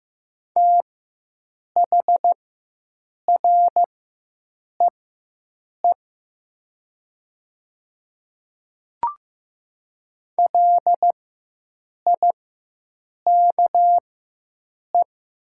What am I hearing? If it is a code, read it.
Morse code, heard as a keyed tone, about 15 wpm, spaced as for 7 wpm, 700 Hz. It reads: THREE LIKE
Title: 4x Character Spacing - Top 200 Words - 15wpm